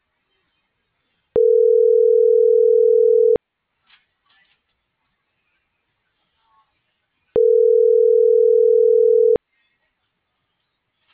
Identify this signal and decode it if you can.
Ringback tone (North America), 440 + 480 Hz.